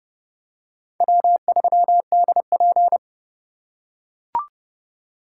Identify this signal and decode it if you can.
Morse code, heard as a keyed tone, about 29 wpm, 700 Hz. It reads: W3DP